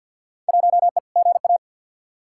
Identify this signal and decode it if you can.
Morse code, heard as a keyed tone, about 51 wpm, 700 Hz. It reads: 1E GA
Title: Field Day Exchanges - 50wpm